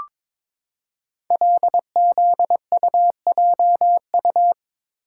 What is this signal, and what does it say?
Morse code, heard as a keyed tone, about 23 wpm, 700 Hz. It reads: LZUJU
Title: Code Groups - 5 Letters - 22wpm